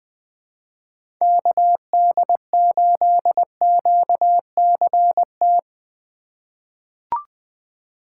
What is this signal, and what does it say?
Morse code, heard as a keyed tone, about 20 wpm, 700 Hz. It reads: KD8QCT
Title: USA Call Signs - 6 Characters - 4x Character Spacing - 20wpm